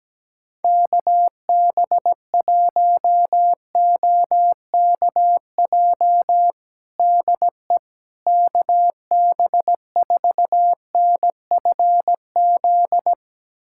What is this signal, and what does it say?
Morse code, heard as a keyed tone, about 17 wpm, 700 Hz. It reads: KB1OKJ DE KB4NFZ
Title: USA Callsign DE Callsign 17wpm